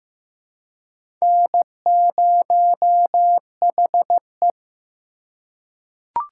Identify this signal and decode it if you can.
Morse code, heard as a keyed tone, about 15 wpm, 700 Hz. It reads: N0HE
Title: USA Call Signs - 4 Characters - 4x Character Spacing - 15wpm